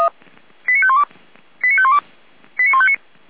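Five-tone selective call (ZVEI), heard as quick runs of five tones, tones of about 70 ms. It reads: 89512 89512 89159
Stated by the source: ZVEI five-tone selective call 89170 received with an SDR.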